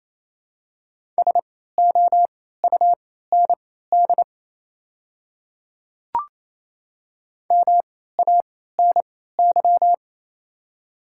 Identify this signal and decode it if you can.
Morse code, heard as a keyed tone, about 28 wpm, spaced as for 16 wpm, 700 Hz. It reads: SOUND MANY
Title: 3x Character Spacing - Top 100 Words - 28wpm